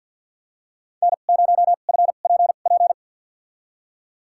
Morse code, heard as a keyed tone, about 53 wpm, spaced as for 37 wpm, 700 Hz. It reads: N0FPP
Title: USA Call Signs - 5 Characters - 2x Character Spacing - 50wpm